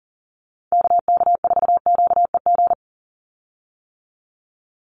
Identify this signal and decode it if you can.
Morse code, heard as a keyed tone, about 40 wpm, 700 Hz. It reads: KK4QEG
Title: USA Call Signs 40wpm